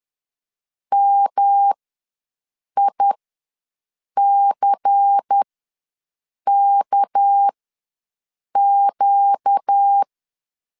Morse code, heard as a keyed tone, about 10 wpm, spaced as for 6 wpm, 800 Hz. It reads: MICKQ
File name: LEZIONI / 7 GF9Z / 7 G ws 10wpm 3843